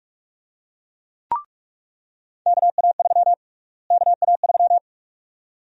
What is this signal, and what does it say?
Morse code, heard as a keyed tone, about 45 wpm, 700 Hz. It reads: KA3 KA3
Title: USA Call Sign Prefixes - Repeated 2x - 45wpm